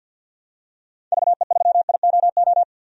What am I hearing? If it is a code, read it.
Morse code, heard as a keyed tone, about 46 wpm, 700 Hz. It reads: VE3IOO
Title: DXCC-100 Call Signs - 6 Characters - 50wpm